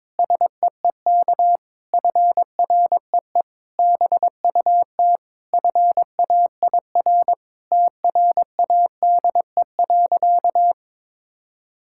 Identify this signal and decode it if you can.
Morse code, heard as a keyed tone, about 22 wpm, 700 Hz. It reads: SEEK FREE BUT FAIR TRADE.
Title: Sentences from Top 1600 Words 22wpm